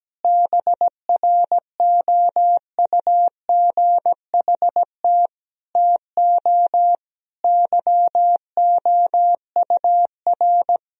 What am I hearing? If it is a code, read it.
Morse code, heard as a keyed tone, about 17 wpm, 700 Hz. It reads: BROUGHT TO YOUR
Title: Sentences from Top 1000 Words - Review Part 2 - 17wpm